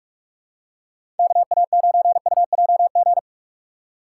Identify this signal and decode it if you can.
Morse code, heard as a keyed tone, about 46 wpm, 690 Hz. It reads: KA9UJG
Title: USA Call Signs - 6 Characters - 3x Character Spacing - 45wpm